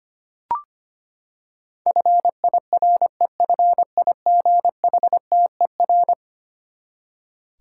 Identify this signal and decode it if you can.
Morse code, heard as a keyed tone, about 25 wpm, 700 Hz. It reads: FIREFIGHTER
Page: morse-code-ninja-practice-25wpm